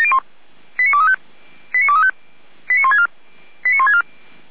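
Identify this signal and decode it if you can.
Five-tone selective call (ZVEI), heard as quick runs of five tones, tones of about 70 ms. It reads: 89235 89235 89164 89164
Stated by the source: ZVEI five-tone call received by SDR.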